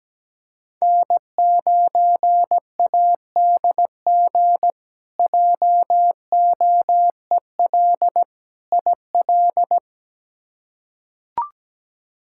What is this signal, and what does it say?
Morse code, heard as a keyed tone, about 17 wpm, 700 Hz. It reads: N9ADG JOEL IL
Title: K1USN SST QSO Practice - 17wpm